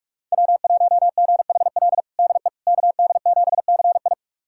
Morse code, heard as a keyed tone, about 46 wpm, 700 Hz. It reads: W1GHL DE KD7CI
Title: USA Callsign DE Callsign - 5 Characters - 45wpm